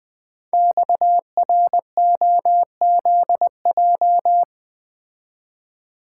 Morse code, heard as a keyed tone, about 20 wpm, 700 Hz. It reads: XROZJ
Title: Code Groups - 5 Letters - 20wpm